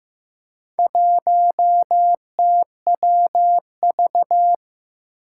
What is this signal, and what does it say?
Morse code, heard as a keyed tone, about 15 wpm, 700 Hz. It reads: 1TWV
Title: Four Challenging Letter-Number Combinations 15wpm